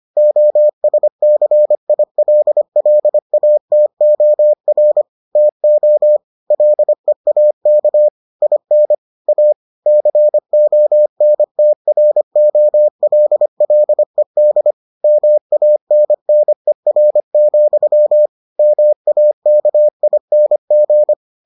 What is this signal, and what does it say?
Morse code, heard as a keyed tone, about 25 wpm, 600 Hz. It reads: OSCILLATOR TO LEAK IN A CONTROLLED MANNER, MAKING